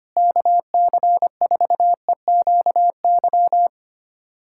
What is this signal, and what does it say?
Morse code, heard as a keyed tone, about 25 wpm, 700 Hz. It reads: KC4EQY
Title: USA Callsign DE Callsign 25wpm